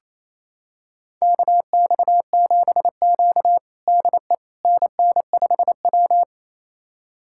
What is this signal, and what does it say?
Morse code, heard as a keyed tone, about 28 wpm, 700 Hz. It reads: KX7Q DE NN5W
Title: USA Callsign DE Callsign - 4 Characters - 28wpm